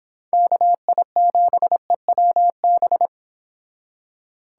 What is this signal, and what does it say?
Morse code, heard as a keyed tone, about 30 wpm, 700 Hz. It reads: KI7EWB